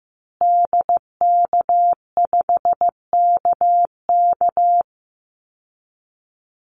Morse code, heard as a keyed tone, about 15 wpm, 700 Hz. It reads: DK5KK
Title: Call Signs Repeated 2x - 15wpm